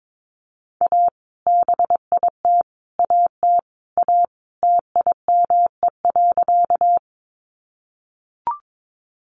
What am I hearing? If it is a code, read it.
Morse code, heard as a keyed tone, about 22 wpm, 700 Hz. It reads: A BIT AT A TIME.
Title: Sentences from Top 900 Words 22wpm